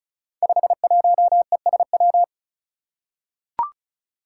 Morse code, heard as a keyed tone, about 36 wpm, 700 Hz. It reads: 51ESW
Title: Five Challenging Letter-Number Combinations 35wpm